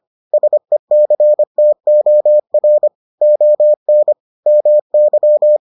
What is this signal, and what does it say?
Morse code, heard as a keyed tone, about 25 wpm, 600 Hz. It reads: SECTOR ON MY